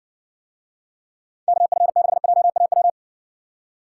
Morse code, heard as a keyed tone, about 60 wpm, 700 Hz. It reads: KF6JAW